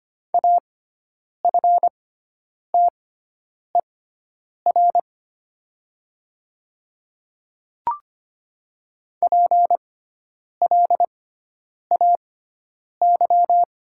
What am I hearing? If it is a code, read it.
Morse code, heard as a keyed tone, about 25 wpm, spaced as for 9 wpm, 700 Hz. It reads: AFTER PLAY
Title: 6x Character Spacing - Top 300 Words - 25wpm